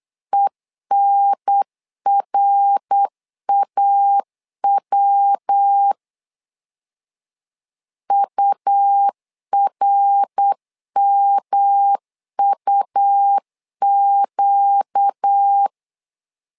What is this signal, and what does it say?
Morse code, heard as a keyed tone, about 8 wpm, 800 Hz. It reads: ENRAW URMUQ
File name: 4 Q 8wpm